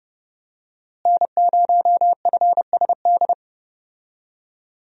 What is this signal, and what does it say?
Morse code, heard as a keyed tone, about 30 wpm, 700 Hz. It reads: N0FSD
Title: USA Call Signs - 5 Characters - Repeated 3x - 30wpm